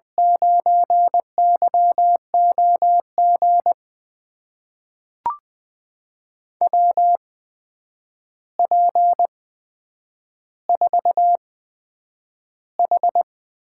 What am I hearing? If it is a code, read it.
Morse code, heard as a keyed tone, about 20 wpm, spaced as for 9 wpm, 700 Hz. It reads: 9YOG WP4H
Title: USA Call Signs - 6 Characters - 8x Character Spacing - 20wpm